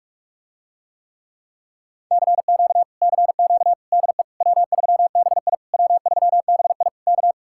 Morse code, heard as a keyed tone, about 45 wpm, 700 Hz. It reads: CQ CQ DE W3BI W3BI K